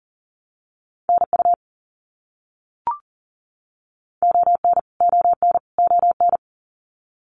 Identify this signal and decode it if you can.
Morse code, heard as a keyed tone, about 40 wpm, 700 Hz. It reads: NU ON ON ON
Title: Canadian Provinces and Territories - Repeated 3x - 40wpm